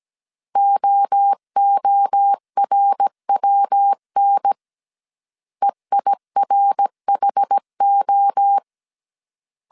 Morse code, heard as a keyed tone, about 17 wpm, 800 Hz. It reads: OORWN EIRHO